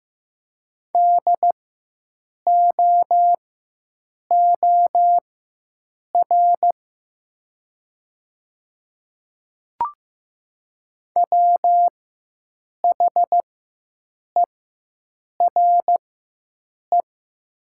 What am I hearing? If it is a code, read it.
Morse code, heard as a keyed tone, about 15 wpm, spaced as for 7 wpm, 700 Hz. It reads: DOOR WHERE
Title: 4x Character Spacing - Top 300 Words - 15wpm